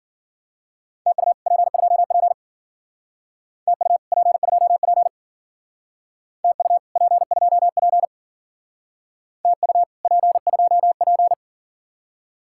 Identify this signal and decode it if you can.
Morse code, heard as a keyed tone, about 47 wpm, 700 Hz. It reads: TU P2P TU P2P TU P2P TU P2P
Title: Speed-Racing Common QSO Phrases - 40wpm